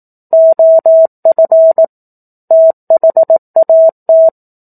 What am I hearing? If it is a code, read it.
Morse code, heard as a keyed tone, about 18 wpm, 650 Hz. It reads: OF THAT
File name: 04-Chapter-04 18 WPM